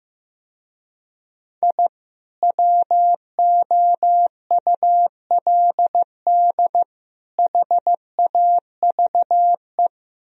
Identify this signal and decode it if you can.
Morse code, heard as a keyed tone, about 16 wpm, 700 Hz. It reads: I WOULD HAVE
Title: Speed-Racing - Sets of 3 Words 15wpm